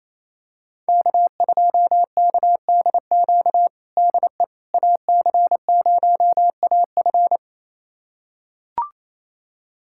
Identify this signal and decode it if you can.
Morse code, heard as a keyed tone, about 28 wpm, 700 Hz. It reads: K2KDQ DE AC0AF
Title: USA Callsign DE Callsign - 5 Characters - 28wpm